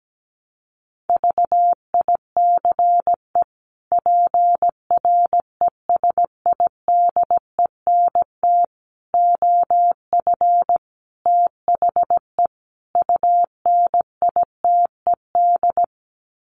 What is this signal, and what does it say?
Morse code, heard as a keyed tone, about 17 wpm, 700 Hz. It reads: VICE PRESIDENT OF THE UNITED
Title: Sets of 5 Words Encore2 17wpm